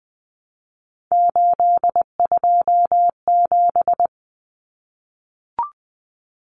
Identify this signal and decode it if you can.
Morse code, heard as a keyed tone, about 20 wpm, 700 Hz. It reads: 827
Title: Common Cut Numbers - 3-Digit 20wpm